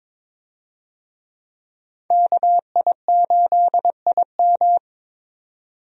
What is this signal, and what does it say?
Morse code, heard as a keyed tone, about 22 wpm, 700 Hz. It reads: KI8IM
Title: USA Call Signs - 5 Characters - 22wpm